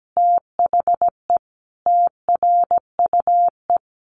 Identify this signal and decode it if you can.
Morse code, heard as a keyed tone, about 17 wpm, 700 Hz. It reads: THE TRUE
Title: Sentences from Top 1400 Words 17wpm